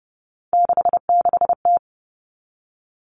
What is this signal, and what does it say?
Morse code, heard as a keyed tone, about 30 wpm, 700 Hz. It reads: 66T
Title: Common Cut Numbers - 3-Digit 30wpm